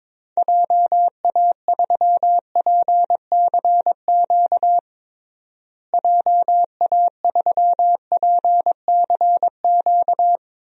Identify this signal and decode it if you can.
Morse code, heard as a keyed tone, about 22 wpm, 700 Hz. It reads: JA3PCQ JA3PCQ
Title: DXCC-100 Call Signs - 6 Characters - Repeated 3x - 22wpm